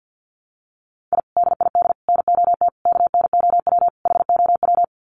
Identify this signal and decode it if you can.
Morse code, heard as a keyed tone, about 50 wpm, 700 Hz. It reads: I DID NOT KNOW HOW
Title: Sets of 5 Words Encore 50wpm